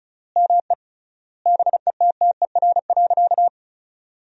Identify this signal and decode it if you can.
Morse code, heard as a keyed tone, about 36 wpm, 700 Hz. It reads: ME BETTER.
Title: 3x Word-Spacing - Sentences from Top 400 Words - 35wpm